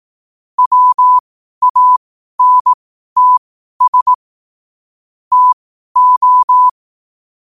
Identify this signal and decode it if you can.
Morse code, heard as a keyed tone, about 18 wpm, spaced as for 12 wpm, 1000 Hz. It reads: WANTS TO
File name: qotdfiles / 12wpm